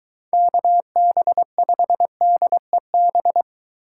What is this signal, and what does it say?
Morse code, heard as a keyed tone, about 22 wpm, 700 Hz. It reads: KB5DEB